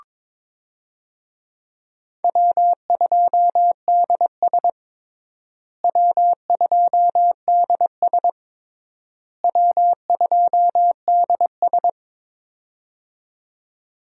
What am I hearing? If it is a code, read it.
Morse code, heard as a keyed tone, about 22 wpm, 700 Hz. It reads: W2DS W2DS W2DS
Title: USA Call Signs - 4 Characters - Repeated 3x - 22wpm